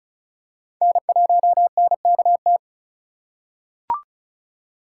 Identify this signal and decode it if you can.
Morse code, heard as a keyed tone, about 34 wpm, 700 Hz. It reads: N1NKT
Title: USA Call Signs - 5 Characters - 4x Character Spacing - 35wpm